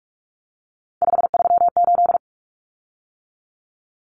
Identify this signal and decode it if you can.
Morse code, heard as a keyed tone, about 44 wpm, 700 Hz. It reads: 538
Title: Common Cut Numbers - 3-Digit 45wpm